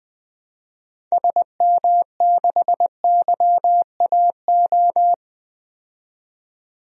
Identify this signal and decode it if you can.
Morse code, heard as a keyed tone, about 20 wpm, 700 Hz. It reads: SM6YAO
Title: DXCC-100 Call Signs - 6 Characters - 20wpm